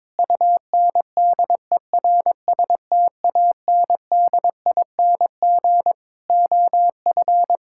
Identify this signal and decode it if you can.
Morse code, heard as a keyed tone, about 22 wpm, 700 Hz. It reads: UNDERSTANDING OF
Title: Sentences from Top 1000 Words - Review Part 6 - 22wpm